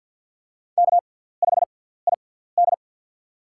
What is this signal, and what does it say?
Morse code, heard as a keyed tone, about 47 wpm, spaced as for 17 wpm, 700 Hz. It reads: K5ID
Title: USA Call Signs - 4 Characters - 6x Character Spacing - 50wpm